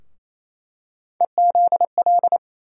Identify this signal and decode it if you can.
Morse code, heard as a keyed tone, about 28 wpm, 700 Hz. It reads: EZL
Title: USA Call Sign Postfixes - 3 Letters 28wpm